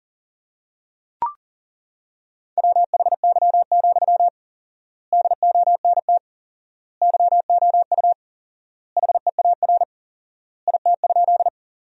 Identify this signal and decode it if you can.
Morse code, heard as a keyed tone, about 37 wpm, 700 Hz. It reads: WHY, DONT YOU HEAR IT?
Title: 4x Word-Spacing - Sentences from Top 300 Words - 40wpm